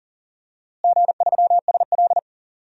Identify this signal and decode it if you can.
Morse code, heard as a keyed tone, about 37 wpm, 700 Hz. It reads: G3SL
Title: DXCC-100 Call Signs - 4 Characters - Repeated 2x - 40wpm